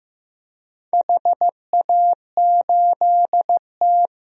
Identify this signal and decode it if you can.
Morse code, heard as a keyed tone, about 15 wpm, 700 Hz. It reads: HA8T
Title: DXCC-100 Call Signs - 4 Characters - 15wpm